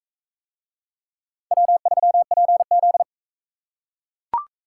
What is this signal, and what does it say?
Morse code, heard as a keyed tone, about 43 wpm, 700 Hz. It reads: W3PZ